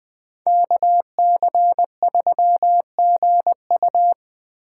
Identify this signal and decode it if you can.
Morse code, heard as a keyed tone, about 20 wpm, 700 Hz. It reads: KC3GU